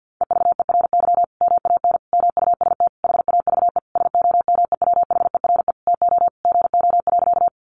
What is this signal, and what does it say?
Morse code, heard as a keyed tone, about 48 wpm, 700 Hz. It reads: EVERY MAN MUST HAVE SOMEWHERE TO GO.
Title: Sentences from Top 1400 Words 50wpm